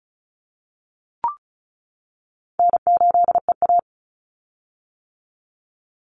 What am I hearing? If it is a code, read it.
Morse code, heard as a keyed tone, about 36 wpm, 700 Hz. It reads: N8EA
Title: Call Signs - 35wpm